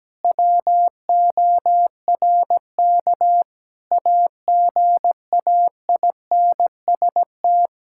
Morse code, heard as a keyed tone, about 17 wpm, 700 Hz. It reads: WORK AGAINST
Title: Sentences from Top 1000 Words - Review Part 5 - 17wpm